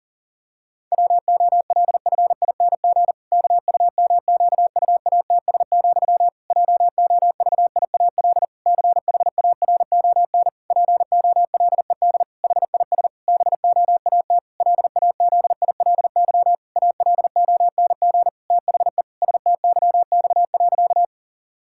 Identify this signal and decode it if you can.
Morse code, heard as a keyed tone, about 40 wpm, 700 Hz. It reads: WOLFING KUMQUATS, JOVIAL CHARON POLED HIS BOAT LAZILY ALONG THE STYX.
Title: Pangram Challenge v2 - 40wpm